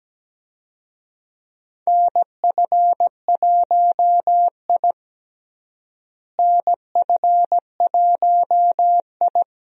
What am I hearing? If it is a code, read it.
Morse code, heard as a keyed tone, about 17 wpm, 700 Hz. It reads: NF1I NF1I
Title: USA Call Signs - 4 Characters - Repeated 2x - 17wpm